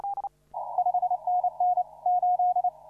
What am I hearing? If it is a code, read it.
Morse code, heard as a keyed tone, about 30 wpm, 720 Hz. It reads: 5NN 8
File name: CW / RHOB CW 2017 HA6P 0001